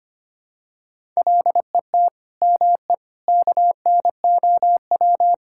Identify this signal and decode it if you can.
Morse code, heard as a keyed tone, about 25 wpm, 700 Hz. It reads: LET ME KNOW
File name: practice / 25wpm / Sentences with 9-Word Phrases - 25wpm